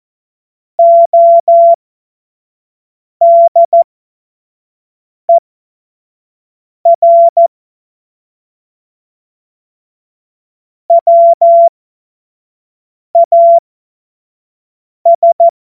Morse code, heard as a keyed tone, about 14 wpm, spaced as for 5 wpm, 680 Hz. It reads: ODER WAS
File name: Morsekurs / Witze / Witz 30 Januar 2024 25 F70